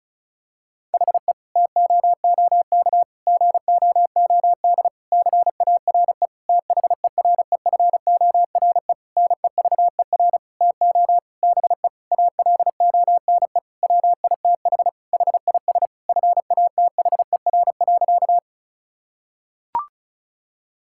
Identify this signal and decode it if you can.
Morse code, heard as a keyed tone, about 34 wpm, 700 Hz. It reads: HE TOOK GOOD CARE THEREFORE NEVER TO BE ALONE WITH HIS FATHER.
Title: Sentences from Top 1000 Words - Review Part 5 - 35wpm